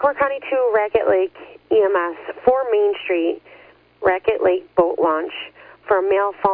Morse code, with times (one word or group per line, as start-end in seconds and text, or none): none